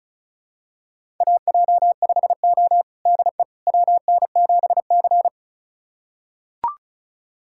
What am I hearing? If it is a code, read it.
Morse code, heard as a keyed tone, about 36 wpm, 700 Hz. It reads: AJ5O DE WN7C
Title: USA Callsign DE Callsign - 4 Characters - 35wpm